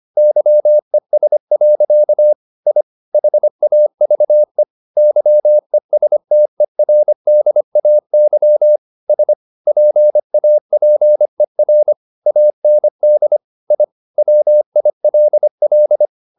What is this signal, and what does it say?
Morse code, heard as a keyed tone, about 25 wpm, 600 Hz. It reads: YES. I HAVE YESTERDAY S PAPER AND I WILL
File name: the old man and the sea part 1 25wpm